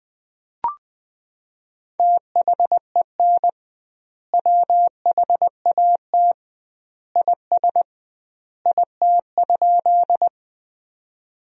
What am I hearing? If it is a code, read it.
Morse code, heard as a keyed tone, about 20 wpm, 700 Hz. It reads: THEN WHAT IS IT?
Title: 2x Word-Spacing - Sentences from Top 100 Words - 20wpm